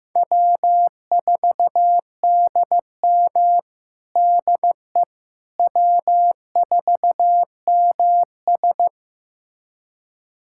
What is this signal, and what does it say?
Morse code, heard as a keyed tone, about 15 wpm, 700 Hz. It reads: W4DM DE W4MS
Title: USA Callsign DE Callsign - 4 Characters - 15wpm